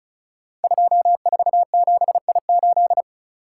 Morse code, heard as a keyed tone, about 36 wpm, 690 Hz. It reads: 247I8
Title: Code Groups - 5 Letter:Numbers - 35wpm